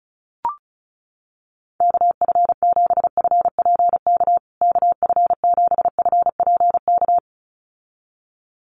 Morse code, heard as a keyed tone, about 34 wpm, 700 Hz. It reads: KF7FPK KF7FPK
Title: USA Call Signs Repeated 2x 35wpm